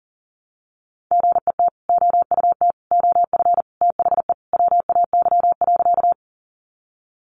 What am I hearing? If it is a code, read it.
Morse code, heard as a keyed tone, about 40 wpm, 700 Hz. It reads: GET OUT OF THE WAY.
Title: Sentences from Top 200 Words 40wpm